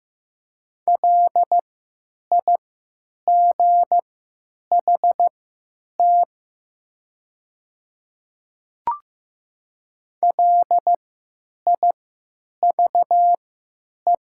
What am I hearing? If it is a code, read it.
Morse code, heard as a keyed tone, about 15 wpm, spaced as for 9 wpm, 700 Hz. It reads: LIGHT LIVE
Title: 3x Character Spacing - Top 200 Words - 15wpm